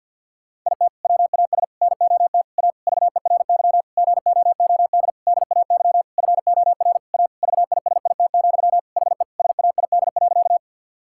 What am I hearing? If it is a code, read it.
Morse code, heard as a keyed tone, about 50 wpm, 700 Hz. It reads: IT WAS NOT A VERY GOOD DAY FOR A VISIT, HE SAID.
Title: Sentences from Top 1000 Words - Review Part 6 - 50wpm